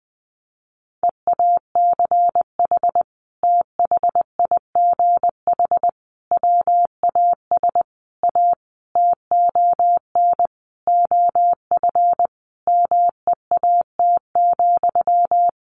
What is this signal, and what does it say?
Morse code, heard as a keyed tone, about 19 wpm, 700 Hz. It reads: EACH THIGH WAS A TON OF MEAT,
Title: A Sound Of Thunder 20wpm